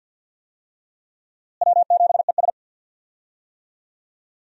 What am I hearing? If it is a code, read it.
Morse code, heard as a keyed tone, about 47 wpm, 700 Hz. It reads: W7ES
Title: USA Call Signs - 4 Characters - Repeated 2x - 50wpm